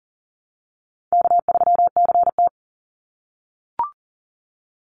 Morse code, heard as a keyed tone, about 37 wpm, 700 Hz. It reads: K3CT
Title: Easy Call Signs - 40wpm